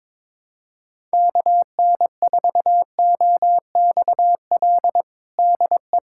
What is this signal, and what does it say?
Morse code, heard as a keyed tone, about 21 wpm, 700 Hz. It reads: KN4OXL DE
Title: USA Callsign DE Callsign 22wpm